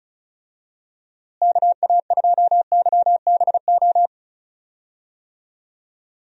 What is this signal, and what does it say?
Morse code, heard as a keyed tone, about 34 wpm, 700 Hz. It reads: KA2YBO
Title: USA Call Signs - 6 Characters - Repeated 2x - 35wpm